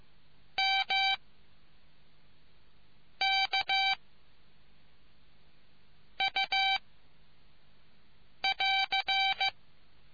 Morse code, heard as a keyed tone, about 15 wpm, 780 Hz. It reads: M K U <AR>